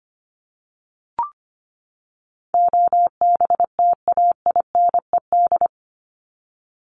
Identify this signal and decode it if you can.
Morse code, heard as a keyed tone, about 25 wpm, 700 Hz. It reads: OBTAINED
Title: ED Suffix - Top 100 Words with Base Words 25wpm